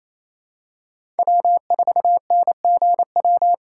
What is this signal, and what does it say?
Morse code, heard as a keyed tone, about 28 wpm, 700 Hz. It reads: W4NGW
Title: USA Call Signs - 3x Character Spacing - 28wpm